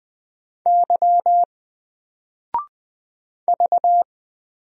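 Morse code, heard as a keyed tone, about 20 wpm, 700 Hz. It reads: Y V
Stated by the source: Single letter and number Morse drill at 20 wpm, from N4LCD.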